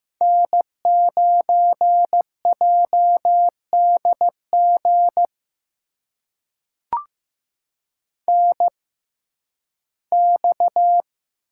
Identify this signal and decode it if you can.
Morse code, heard as a keyed tone, about 16 wpm, spaced as for 11 wpm, 700 Hz. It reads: N9JDG NX
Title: USA Call Signs - 6x Character Spacing - 15wpm